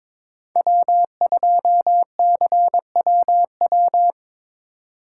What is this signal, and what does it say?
Morse code, heard as a keyed tone, about 21 wpm, 690 Hz. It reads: W2CWW
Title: USA Call Signs - 5 Characters - 2x Character Spacing - 22wpm